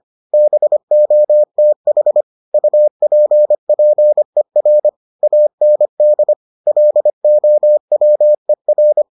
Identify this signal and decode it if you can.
Morse code, heard as a keyed tone, about 25 wpm, 600 Hz. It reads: BOTH UPPER AND LOWER